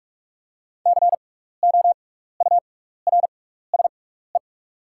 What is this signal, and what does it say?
Morse code, heard as a keyed tone, about 48 wpm, spaced as for 16 wpm, 700 Hz. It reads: COURSE
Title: 6x Character Spacing - Top 500 Words - 45wpm